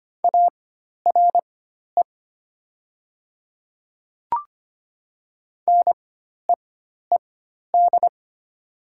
Morse code, heard as a keyed tone, about 25 wpm, spaced as for 12 wpm, 700 Hz. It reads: ARE NEED
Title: 4x Character Spacing - Top 200 Words - 25wpm